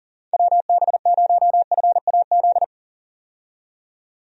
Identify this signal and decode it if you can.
Morse code, heard as a keyed tone, about 40 wpm, 700 Hz. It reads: WB0FAZ
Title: USA Call Signs - 6 Characters - 40wpm